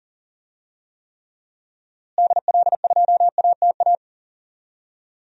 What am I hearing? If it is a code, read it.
Morse code, heard as a keyed tone, about 40 wpm, 700 Hz. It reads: DL2ATA
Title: DXCC-100 Call Signs - 6 Characters - Repeated 2x - 40wpm